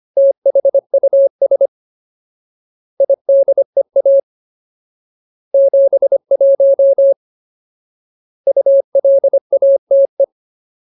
Morse code, heard as a keyed tone, about 25 wpm, 550 Hz. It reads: THUS IDEA 71 ULATE
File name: Ultimate Word List / 25WPM Ultimate